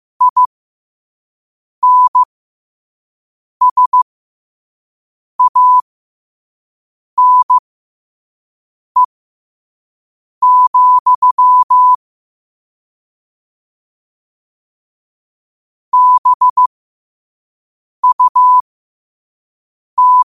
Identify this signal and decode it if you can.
Morse code, heard as a keyed tone, about 15 wpm, spaced as for 5 wpm, 1000 Hz. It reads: INSANE, BUT
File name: qotdfiles / 5wpm